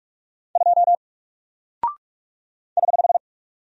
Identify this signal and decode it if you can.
Morse code, heard as a keyed tone, about 44 wpm, 700 Hz. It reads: 2 <HH>
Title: Single Character (Letter:Number:Punctuation:Prosign) 45wpm